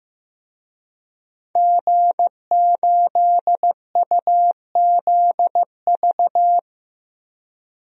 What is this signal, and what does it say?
Morse code, heard as a keyed tone, about 15 wpm, 700 Hz. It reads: G8UZV
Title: DXCC-100 Call Signs - 5 Characters - 15wpm